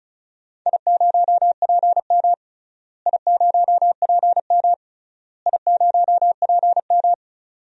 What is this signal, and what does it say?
Morse code, heard as a keyed tone, about 36 wpm, 700 Hz. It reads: I0PM I0PM I0PM
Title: DXCC-100 Call Signs - 4 Characters - Repeated 3x - 35wpm